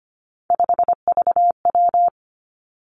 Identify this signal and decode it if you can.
Morse code, heard as a keyed tone, about 25 wpm, 700 Hz. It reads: 5VW
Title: Three Challenging Letter-Number Combinations 25wpm